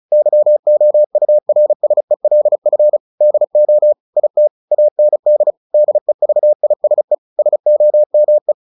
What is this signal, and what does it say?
Morse code, heard as a keyed tone, about 34 wpm, 600 Hz. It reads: YOURSELF DO IT AND DEVISE SOME
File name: the old man and the sea 2 35wpm